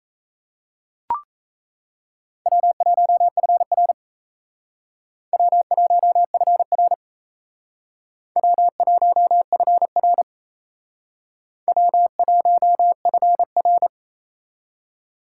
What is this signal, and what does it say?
Morse code, heard as a keyed tone, about 35 wpm, 700 Hz. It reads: W1FR W1FR W1FR W1FR
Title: Speed Racing - USA 1x2 Call Signs 28wpm